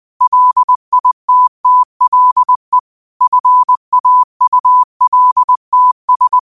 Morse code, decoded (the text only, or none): LITTLE FAULTS